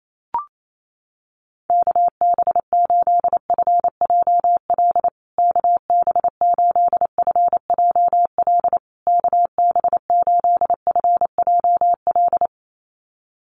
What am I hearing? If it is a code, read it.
Morse code, heard as a keyed tone, about 28 wpm, 700 Hz. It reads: KB8FJL KB8FJL KB8FJL
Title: USA Call Signs Repeated 3x 28wpm